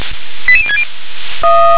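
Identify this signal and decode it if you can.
Five-tone selective call (ZVEI), heard as quick runs of five tones, tones of about 70 ms. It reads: 88060
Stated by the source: ZVEI five-tone call received by SDR.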